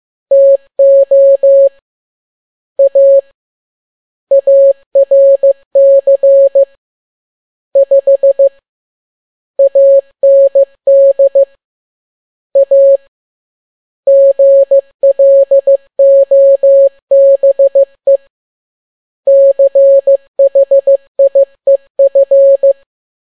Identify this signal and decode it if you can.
Morse code, heard as a keyed tone, about 15 wpm, 550 Hz. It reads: TO A ARC 5 AND A GLOBE CHIEF
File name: n6tt p1 15C13